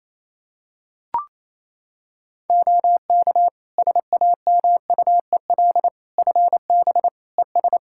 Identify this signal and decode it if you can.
Morse code, heard as a keyed tone, about 28 wpm, 700 Hz. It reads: OK SAMUEL FB ES
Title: Practice QSO - Exchange 2 - Info - 28wpm